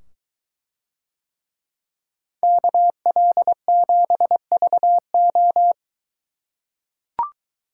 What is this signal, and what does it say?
Morse code, heard as a keyed tone, about 23 wpm, 700 Hz. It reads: KL7VO